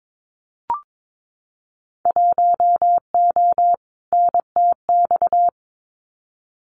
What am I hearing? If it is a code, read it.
Morse code, heard as a keyed tone, about 22 wpm, 700 Hz. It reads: 1O NTX
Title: Winter Field Day Exchanges 22wpm